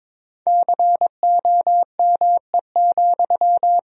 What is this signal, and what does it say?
Morse code, heard as a keyed tone, about 22 wpm, 700 Hz. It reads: COME,
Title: Sentences from Top 1000 Words - Review Part 2 - 22wpm